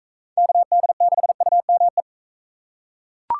Morse code, heard as a keyed tone, about 40 wpm, 700 Hz. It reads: KD6UME